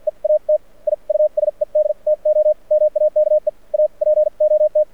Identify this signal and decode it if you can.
Morse code, heard as a keyed tone, about 57 wpm, spaced as for 45 wpm, 590 Hz. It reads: EAT I USED TO MAKE A POT
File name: archived / CRSnet